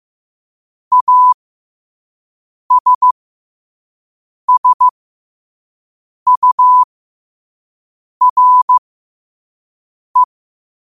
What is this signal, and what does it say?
Morse code, heard as a keyed tone, about 15 wpm, spaced as for 5 wpm, 1000 Hz. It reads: ASSURE
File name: qotdfiles / 5wpm